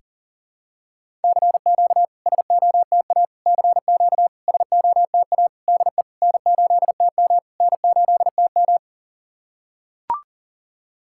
Morse code, heard as a keyed tone, about 40 wpm, 700 Hz. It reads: CQ SOTA CQ SOTA DE N8TM N8TM
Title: SOTA QSO Practice 40wpm